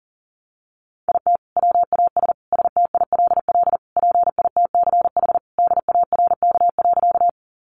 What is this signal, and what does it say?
Morse code, heard as a keyed tone, about 40 wpm, 700 Hz. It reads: IT WAS STILL PITCH DARK.
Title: Sentences from Top 1300 Words 40wpm